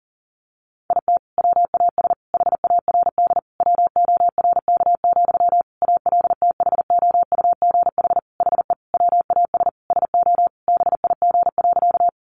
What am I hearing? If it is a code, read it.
Morse code, heard as a keyed tone, about 40 wpm, 700 Hz. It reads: IT WAS HARD WORK, ALTHOUGH HE WAS SO BIG.
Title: Sentences from Top 600 Words 40wpm